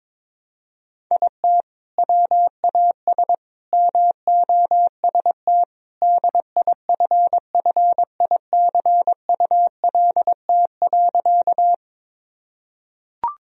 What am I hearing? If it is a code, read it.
Morse code, heard as a keyed tone, about 21 wpm, 700 Hz. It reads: IT WAS MOST DIFFICULT.
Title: Sentences from Top 1000 Words - Review Part 1 - 22wpm